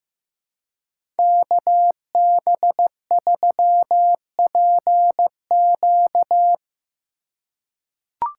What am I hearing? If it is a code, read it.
Morse code, heard as a keyed tone, about 15 wpm, 700 Hz. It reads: KB3PQ
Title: USA Call Signs - 5 Characters - 4x Character Spacing - 15wpm